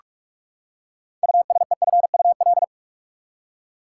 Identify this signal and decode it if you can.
Morse code, heard as a keyed tone, about 46 wpm, 700 Hz. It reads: USEFUL
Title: Six Letter Words - 45wpm